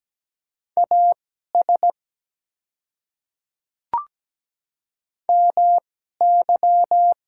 Morse code, heard as a keyed tone, about 17 wpm, spaced as for 12 wpm, 700 Hz. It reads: AS MY
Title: 2x Character Spacing - Top 200 Words - 17wpm